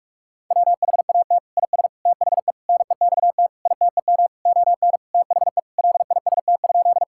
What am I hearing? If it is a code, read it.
Morse code, heard as a keyed tone, about 46 wpm, 700 Hz. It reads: WHAT IS THE NEXT ITEM ON THE LIST?
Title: Sentences from Top 1000 Words - Review Part 6 - 45wpm